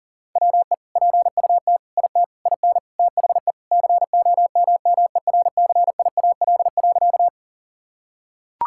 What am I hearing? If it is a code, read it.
Morse code, heard as a keyed tone, about 40 wpm, 700 Hz. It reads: WE PUT IT IN THE COMMERCIAL.
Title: Sentences from Top 1000 Words - Review Part 5 - 40wpm